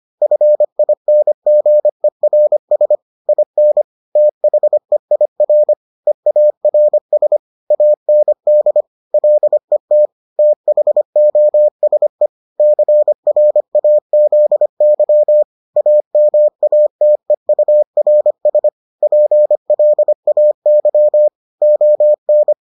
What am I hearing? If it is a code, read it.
Morse code, heard as a keyed tone, about 25 wpm, 600 Hz. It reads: FINGERS IN THEIR EARS AND LET THOSE CRAZY AMATEURS PLAY ON